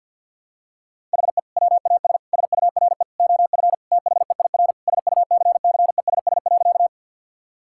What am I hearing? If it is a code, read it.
Morse code, heard as a keyed tone, about 50 wpm, 700 Hz. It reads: HE WAS SURE OF THEIR SUCCESS.
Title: Sentences from Top 1000 Words - Review Part 6 - 50wpm